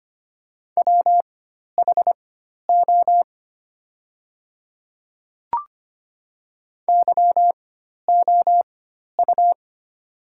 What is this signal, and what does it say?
Morse code, heard as a keyed tone, about 25 wpm, spaced as for 12 wpm, 700 Hz. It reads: WHO YOU
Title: 4x Character Spacing - Top 100 Words - 25wpm